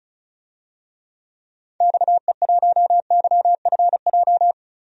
Morse code, heard as a keyed tone, about 36 wpm, 700 Hz. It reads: XE1YFJ
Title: DXCC-100 Call Signs - 6 Characters - 35wpm